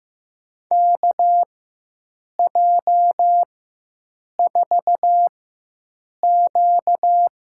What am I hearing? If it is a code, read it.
Morse code, heard as a keyed tone, about 15 wpm, spaced as for 7 wpm, 700 Hz. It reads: KJ4Q
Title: USA Call Signs - 6 Characters - 4x Character Spacing - 15wpm